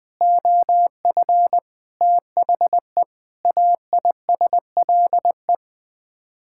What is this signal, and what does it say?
Morse code, heard as a keyed tone, about 20 wpm, 700 Hz. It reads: OF THE AISLE
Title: Sets of 6 Words - 20wpm